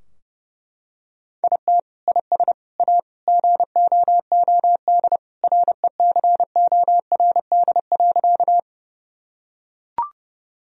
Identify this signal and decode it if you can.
Morse code, heard as a keyed tone, about 32 wpm, 700 Hz. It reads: IT IS A GOOD RECORD.
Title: Sentences from Top 1000 Words - Review Part 3 - 30wpm